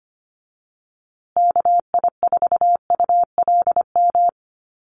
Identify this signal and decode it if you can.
Morse code, heard as a keyed tone, about 25 wpm, 700 Hz. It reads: KI4ULM
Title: USA Call Signs Repeated 3x 25wpm